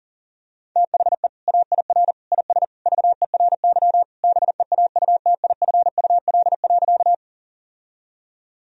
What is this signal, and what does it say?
Morse code, heard as a keyed tone, about 40 wpm, 700 Hz. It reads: THE AIR IS VERY BEAUTIFUL.
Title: Sentences from Top 1000 Words - Review Part 1 - 40wpm